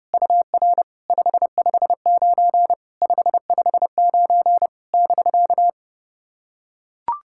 Morse code, heard as a keyed tone, about 30 wpm, 700 Hz. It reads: UR 559 559 <BK>
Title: SOTA QSO Practice 30wpm